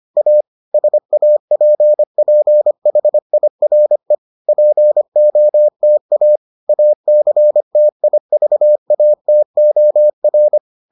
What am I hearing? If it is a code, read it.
Morse code, heard as a keyed tone, about 25 wpm, 600 Hz. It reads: A SAPPHIRE POTA ACTIVATOR